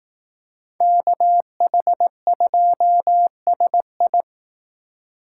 Morse code, heard as a keyed tone, about 18 wpm, 700 Hz. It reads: KH2SI
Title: Speed Racing - USA 2x2 Call Signs 15wpm